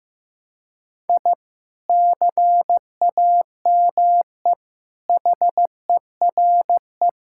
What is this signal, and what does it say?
Morse code, heard as a keyed tone, about 15 wpm, 700 Hz. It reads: I CAME HERE
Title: Sentences from Top 1000 Words - Review Part 6 - 15wpm